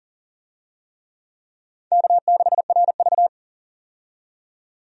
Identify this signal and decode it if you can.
Morse code, heard as a keyed tone, about 40 wpm, 700 Hz. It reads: K6RV
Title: USA Call Signs - 4 Characters - 40wpm